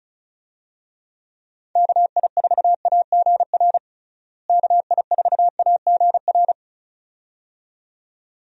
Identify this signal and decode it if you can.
Morse code, heard as a keyed tone, about 36 wpm, 700 Hz. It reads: KI4AGR KI4AGR
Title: USA Call Signs - 6 Characters - Repeated 2x - 35wpm